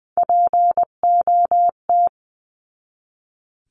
Morse code, heard as a keyed tone, about 20 wpm, 700 Hz. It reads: POT
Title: 259 Lesson 42d - Words 18-20wpm